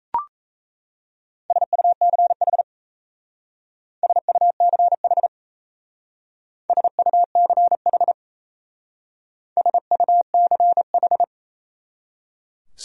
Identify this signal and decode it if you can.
Morse code, heard as a keyed tone, about 35 wpm, 700 Hz. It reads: SUCH SUCH SUCH SUCH
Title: Speed-Racing Top 300 Words 28wpm